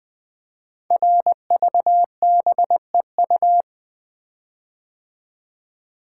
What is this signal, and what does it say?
Morse code, heard as a keyed tone, about 20 wpm, 700 Hz. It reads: RVBEU
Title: Code Groups - 5 Letters - 20wpm